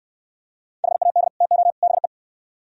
Morse code, heard as a keyed tone, about 68 wpm, 700 Hz. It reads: HAD TO BE